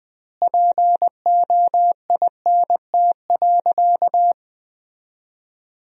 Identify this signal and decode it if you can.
Morse code, heard as a keyed tone, about 20 wpm, 700 Hz. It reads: POINT.
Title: 2x Word-Spacing - Sentences from Top 200 Words - 20wpm